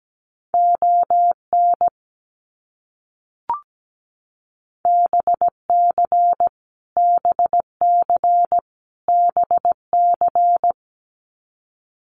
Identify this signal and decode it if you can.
Morse code, heard as a keyed tone, about 17 wpm, 700 Hz. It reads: ON BC BC BC
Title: Canadian Provinces and Territories - Repeated 3x - 17wpm